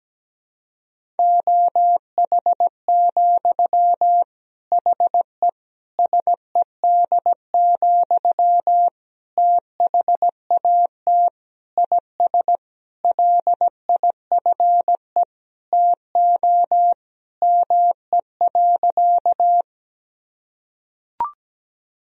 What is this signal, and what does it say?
Morse code, heard as a keyed tone, about 17 wpm, 700 Hz. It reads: OH, HE SED, THAT IS LIFE TO ME.
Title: Sentences from Top 500 Words and Abbreviations 17wpm